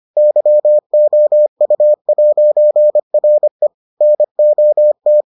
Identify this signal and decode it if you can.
Morse code, heard as a keyed tone, about 25 wpm, 600 Hz. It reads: YOU'RE NOT